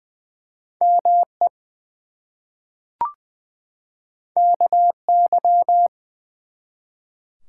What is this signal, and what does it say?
Morse code, heard as a keyed tone, about 20 wpm, 700 Hz. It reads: ME KY
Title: US States and Canadian Provinces 20wpm